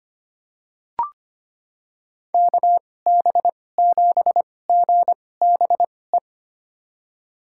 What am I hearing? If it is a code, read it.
Morse code, heard as a keyed tone, about 25 wpm, spaced as for 18 wpm, 700 Hz. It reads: KB7GBE
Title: USA Call Signs - 6 Characters - 2x Character Spacing - 25wpm